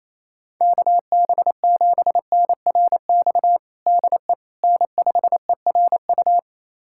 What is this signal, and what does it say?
Morse code, heard as a keyed tone, about 27 wpm, 700 Hz. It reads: KB7NRX DE N5ERU
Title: USA Callsign DE Callsign 28wpm